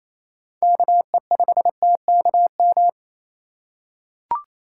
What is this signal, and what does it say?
Morse code, heard as a keyed tone, about 28 wpm, 700 Hz. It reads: KE5TKM
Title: USA Call Signs - 6 Characters - 4x Character Spacing - 28wpm